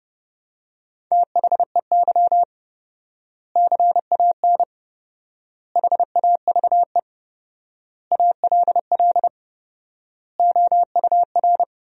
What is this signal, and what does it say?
Morse code, heard as a keyed tone, about 30 wpm, 700 Hz. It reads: THEY CAN HAVE ALL OUR
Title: 4x Word-Spacing - Sentences from Top 400 Words - 30wpm